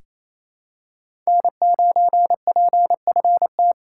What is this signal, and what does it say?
Morse code, heard as a keyed tone, about 28 wpm, 700 Hz. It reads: N9PFT
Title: USA Call Signs - 3x Character Spacing - 28wpm